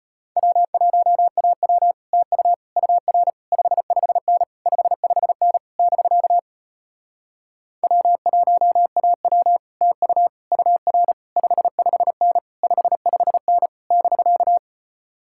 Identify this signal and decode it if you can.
Morse code, heard as a keyed tone, about 33 wpm, 700 Hz. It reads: W1AW TU UR 55N 55N <BK> W1AW TU UR 55N 55N <BK>